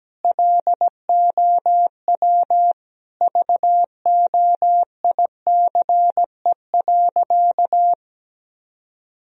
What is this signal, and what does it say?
Morse code, heard as a keyed tone, about 17 wpm, 700 Hz. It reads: LOW VOICE.
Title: Sentences from Top 1000 Words - Review Part 1 - 17wpm